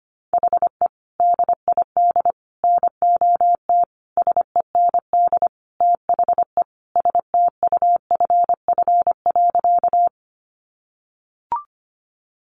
Morse code, heard as a keyed tone, about 25 wpm, 700 Hz. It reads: HE DID NOT SEND THE STUFF.
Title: Sentences from Top 1500 Words 25wpm